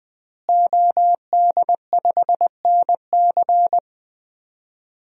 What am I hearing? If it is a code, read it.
Morse code, heard as a keyed tone, about 20 wpm, 700 Hz. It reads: OD5NC
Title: DXCC-100 Call Signs - 5 Characters - Repeated 2x - 20wpm